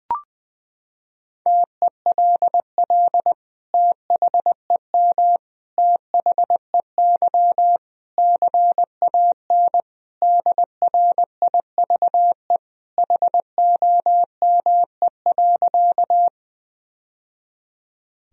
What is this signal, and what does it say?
Morse code, heard as a keyed tone, about 19 wpm, 700 Hz. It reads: TELL THEM THEY CAN DRIVE HOME.
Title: Sentences from Top 1000 Words - Review Part 2 - 20wpm